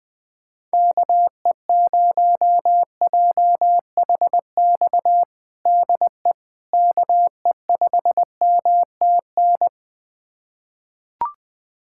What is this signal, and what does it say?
Morse code, heard as a keyed tone, about 20 wpm, 700 Hz. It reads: KE0JHX DE KE5MTN
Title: USA Callsign DE Callsign - 6 Characters - 20wpm